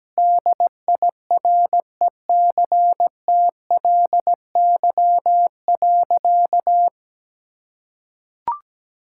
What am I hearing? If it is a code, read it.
Morse code, heard as a keyed tone, about 17 wpm, 700 Hz. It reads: DIRECTLY.
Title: Sentences from Top 1000 Words - Review Part 5 - 17wpm